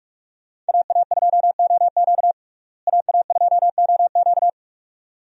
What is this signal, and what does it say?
Morse code, heard as a keyed tone, about 46 wpm, 700 Hz. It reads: AA2OQ AA2OQ
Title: USA Call Signs - 5 Characters - Repeated 2x - 45wpm